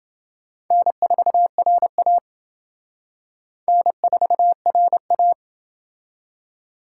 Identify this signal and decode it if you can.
Morse code, heard as a keyed tone, about 28 wpm, 700 Hz. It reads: N4RA N4RA